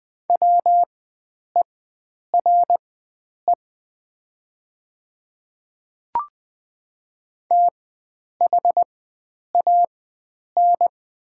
Morse code, heard as a keyed tone, about 20 wpm, spaced as for 9 wpm, 700 Hz. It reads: WERE THAN
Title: 4x Character Spacing - Top 100 Words - 20wpm